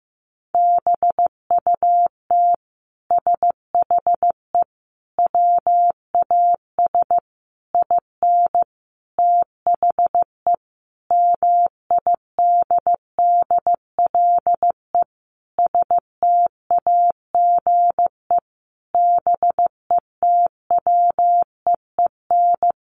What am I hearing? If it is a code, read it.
Morse code, heard as a keyed tone, about 14 wpm, 700 Hz. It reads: BUT SHE WAS IN THE MIDDLE STAGE BETWEEN
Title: Sentences from Top 1200 Words 15wpm